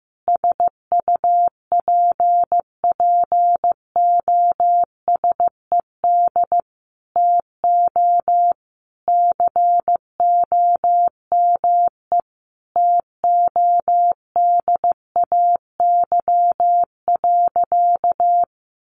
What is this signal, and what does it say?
Morse code, heard as a keyed tone, about 15 wpm, 700 Hz. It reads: SUPPOSED TO COME TODAY.
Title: Easy Phrases 15wpm